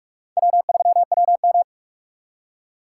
Morse code, heard as a keyed tone, about 58 wpm, 700 Hz. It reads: W3WM